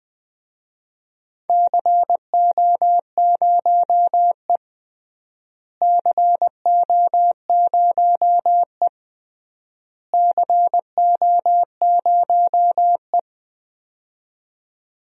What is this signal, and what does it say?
Morse code, heard as a keyed tone, about 20 wpm, 700 Hz. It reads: CO0E CO0E CO0E
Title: DXCC-100 Call Signs - 4 Characters - Repeated 3x - 20wpm